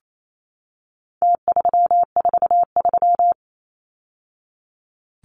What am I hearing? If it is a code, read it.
Morse code, heard as a keyed tone, about 28 wpm, 700 Hz. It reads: T343
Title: Numbers - Contest Style - 4-Digit 28wpm